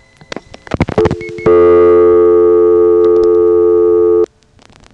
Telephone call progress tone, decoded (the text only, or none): dial tone